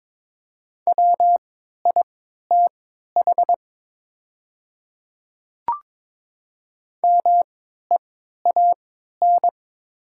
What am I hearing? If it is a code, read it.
Morse code, heard as a keyed tone, about 22 wpm, spaced as for 13 wpm, 700 Hz. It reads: WITH MEAN
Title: 3x Character Spacing - Top 200 Words - 22wpm